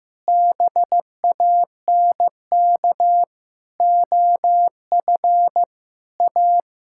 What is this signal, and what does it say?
Morse code, heard as a keyed tone, about 15 wpm, 700 Hz. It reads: BANK OF A
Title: Sentences from Top 1000 Words - Review Part 6 - 15wpm